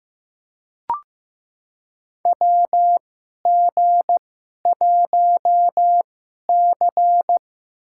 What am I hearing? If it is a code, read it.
Morse code, heard as a keyed tone, about 15 wpm, spaced as for 11 wpm, 700 Hz. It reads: WG1C
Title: USA Call Signs - 4 Characters - 2x Character Spacing - 15wpm